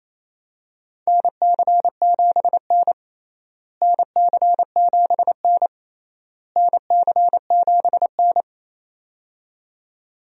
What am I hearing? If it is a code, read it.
Morse code, heard as a keyed tone, about 28 wpm, 700 Hz. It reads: NC7N NC7N NC7N
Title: USA Call Signs - 4 Characters - Repeated 3x - 28wpm